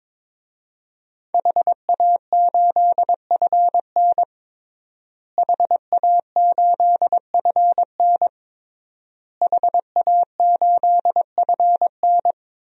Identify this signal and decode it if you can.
Morse code, heard as a keyed tone, about 22 wpm, 700 Hz. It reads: HA8FN HA8FN HA8FN
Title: DXCC-100 Call Signs - 5 Characters - Repeated 3x - 22wpm